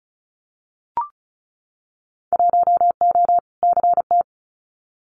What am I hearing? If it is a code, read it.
Morse code, heard as a keyed tone, about 36 wpm, 700 Hz. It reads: 1O CT
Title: Winter Field Day Exchanges 35wpm